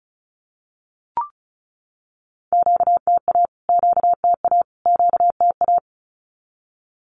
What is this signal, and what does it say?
Morse code, heard as a keyed tone, about 34 wpm, 700 Hz. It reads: QTA QTA QTA
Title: Q-Codes Repeated 3x 35wpm